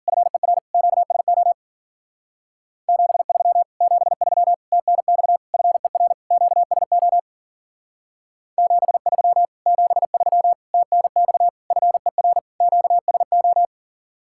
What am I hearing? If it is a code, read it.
Morse code, heard as a keyed tone, about 49 wpm, 700 Hz. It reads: FER QSO 73 73 TNX FER QSO 73 73 TNX FER QSO
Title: Speed-Racing Common QSO Phrases Encore - 40wpm